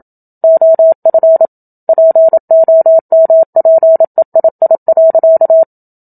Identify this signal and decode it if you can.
Morse code, heard as a keyed tone, about 27 wpm, 650 Hz. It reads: OF POMPEII.
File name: Book Two / 01-Chapter 1 - 27 WPM